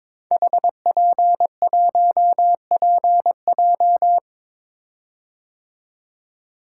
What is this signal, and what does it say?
Morse code, heard as a keyed tone, about 22 wpm, 700 Hz. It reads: HP1PJ
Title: Code Groups - 5 Letter:Numbers - 22wpm